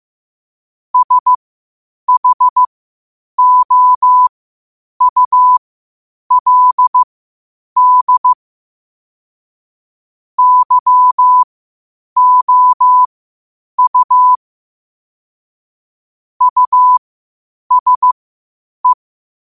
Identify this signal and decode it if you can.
Morse code, heard as a keyed tone, about 15 wpm, spaced as for 8 wpm, 1000 Hz. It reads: SHOULD YOU USE
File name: 8wpm